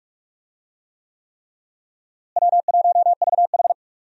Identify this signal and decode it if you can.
Morse code, heard as a keyed tone, about 46 wpm, 700 Hz. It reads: W1VH
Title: USA Call Signs - 4 Characters - Repeated 3x - 45wpm